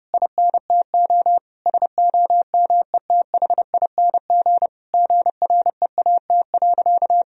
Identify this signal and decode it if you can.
Morse code, heard as a keyed tone, about 32 wpm, 700 Hz. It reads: INTO SOMETHING GREAT.
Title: Sentences from Top 1000 Words - Review Part 2 - 30wpm